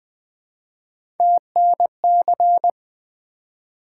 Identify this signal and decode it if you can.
Morse code, heard as a keyed tone, about 20 wpm, 700 Hz. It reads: TNC
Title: Three Letter-Number Combinations - 20wpm